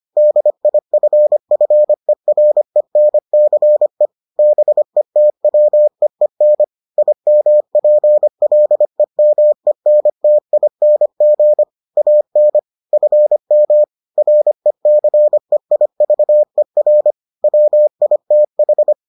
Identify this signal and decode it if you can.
Morse code, heard as a keyed tone, about 25 wpm, 600 Hz. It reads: DIFFERENCE BETWEEN IMPLEMENTING AN FM RECEIVER WITH